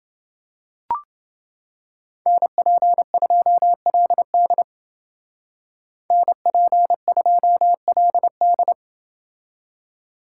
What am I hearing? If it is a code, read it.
Morse code, heard as a keyed tone, about 29 wpm, 700 Hz. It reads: NP2LD NP2LD